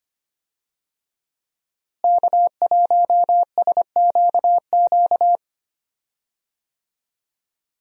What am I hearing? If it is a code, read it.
Morse code, heard as a keyed tone, about 25 wpm, 700 Hz. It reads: K1SQQ
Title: USA Call Signs - 5 Characters - Repeated 3x - 25wpm